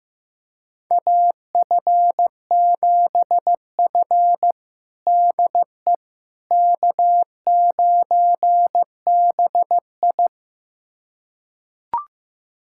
Morse code, heard as a keyed tone, about 15 wpm, 700 Hz. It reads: AF7F DE K9BI
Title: USA Callsign DE Callsign - 4 Characters - 15wpm